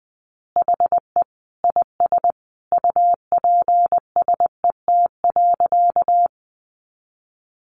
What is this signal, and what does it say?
Morse code, heard as a keyed tone, about 20 wpm, 700 Hz. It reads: HE IS UPSET.
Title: Additional Easy Phrases 20wpm